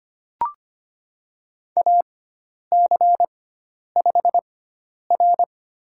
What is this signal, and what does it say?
Morse code, heard as a keyed tone, about 25 wpm, spaced as for 10 wpm, 700 Hz. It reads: AC5R
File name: Lesson 16g - Letter c - Call Signs 25wpm